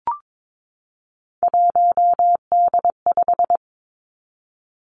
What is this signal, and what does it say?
Morse code, heard as a keyed tone, about 22 wpm, 700 Hz. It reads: 1D5
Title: Three Challenging Letter-Number Combinations 22wpm